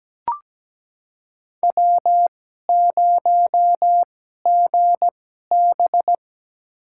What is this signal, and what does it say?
Morse code, heard as a keyed tone, about 17 wpm, spaced as for 12 wpm, 700 Hz. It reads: W0GB
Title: USA Call Signs - 2x Character Spacing - 17wpm